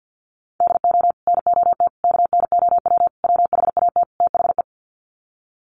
Morse code, heard as a keyed tone, about 51 wpm, 700 Hz. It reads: DO NOT KNOW WHAT THE
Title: Sets of 5 Words 50wpm